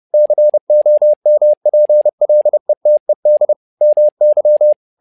Morse code, heard as a keyed tone, about 30 wpm, 600 Hz. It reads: COMPLETED MY
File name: learning center / tower trouble / tower trouble 30wpm